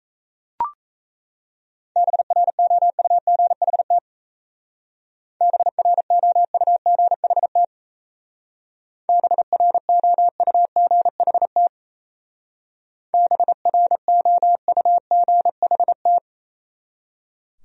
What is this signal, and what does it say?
Morse code, heard as a keyed tone, about 35 wpm, 700 Hz. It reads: BROUGHT BROUGHT BROUGHT BROUGHT